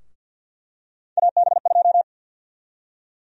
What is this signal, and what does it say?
Morse code, heard as a keyed tone, about 52 wpm, 700 Hz. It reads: AB2
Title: USA Call Sign Prefixes - Repeated 2x - 50wpm